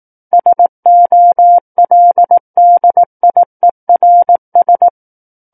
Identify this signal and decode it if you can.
Morse code, heard as a keyed tone, about 18 wpm, 700 Hz. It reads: SOLDIERS
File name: 18 WPM